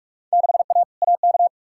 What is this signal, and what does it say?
Morse code, heard as a keyed tone, about 42 wpm, 700 Hz. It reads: 6A AK
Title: Field Day Exchanges Encore - 45wpm